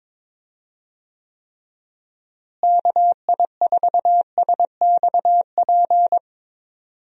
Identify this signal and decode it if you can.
Morse code, heard as a keyed tone, about 22 wpm, 700 Hz. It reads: KI4SXP
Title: USA Call Signs - 6 Characters - Repeated 3x - 22wpm